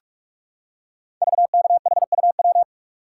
Morse code, heard as a keyed tone, about 44 wpm, 700 Hz. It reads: VKHUW
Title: Code Groups - 5 Letters - 45wpm